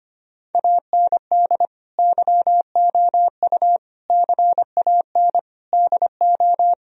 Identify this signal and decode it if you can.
Morse code, heard as a keyed tone, about 25 wpm, 700 Hz. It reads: AND YOU CAN DO
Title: Sentences from Top 1000 Words - Review Part 1 - 25wpm